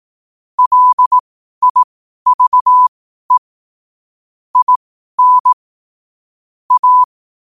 Morse code, heard as a keyed tone, about 18 wpm, spaced as for 12 wpm, 1000 Hz. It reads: LIVE IN A
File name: qotdfiles / 12wpm